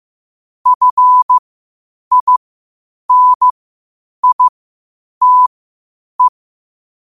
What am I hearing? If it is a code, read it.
Morse code, heard as a keyed tone, about 15 wpm, spaced as for 8 wpm, 1000 Hz. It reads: FINITE
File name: qotdfiles / 8wpm